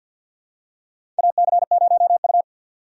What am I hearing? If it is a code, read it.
Morse code, heard as a keyed tone, about 47 wpm, 700 Hz. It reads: AC0U
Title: USA Call Signs - 4 Characters - 6x Character Spacing - 50wpm